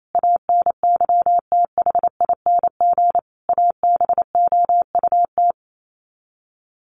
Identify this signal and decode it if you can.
Morse code, heard as a keyed tone, about 28 wpm, 700 Hz. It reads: ANYTHING ABOUT
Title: Sets of 5 Words Encore2 28wpm